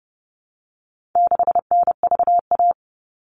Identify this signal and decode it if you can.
Morse code, heard as a keyed tone, about 30 wpm, 700 Hz. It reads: 6NVA
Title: All Cut Numbers - 4-Digit 30wpm